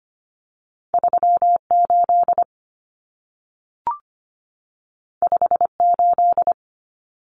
Morse code, heard as a keyed tone, about 25 wpm, 700 Hz. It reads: 38 58
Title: Common Cut Numbers - 2-Digit 25wpm